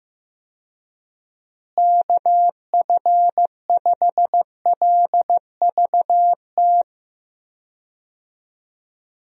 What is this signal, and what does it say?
Morse code, heard as a keyed tone, about 15 wpm, 700 Hz. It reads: KF5LVT
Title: USA Call Signs - 6 Characters - Repeated 3x - 15wpm